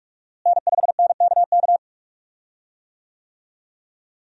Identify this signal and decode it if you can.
Morse code, heard as a keyed tone, about 47 wpm, 700 Hz. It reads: N5NKK
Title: USA Call Signs - 5 Characters - Repeated 2x - 45wpm